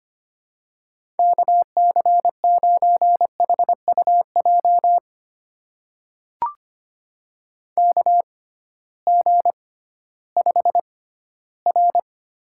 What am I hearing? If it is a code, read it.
Morse code, heard as a keyed tone, about 25 wpm, spaced as for 14 wpm, 700 Hz. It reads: KC9HUJ KG5R
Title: USA Call Signs - 6x Character Spacing - 25wpm